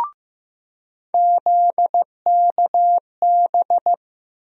Morse code, heard as a keyed tone, about 16 wpm, 700 Hz. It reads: ZKB
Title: USA Call Sign Postfixes - 3 Letters 15wpm